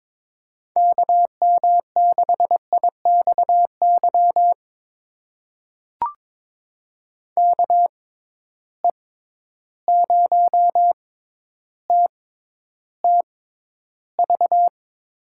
Morse code, heard as a keyed tone, about 21 wpm, spaced as for 11 wpm, 700 Hz. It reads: KM6IXY KE0TTV
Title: USA Call Signs - 6 Characters - 6x Character Spacing - 22wpm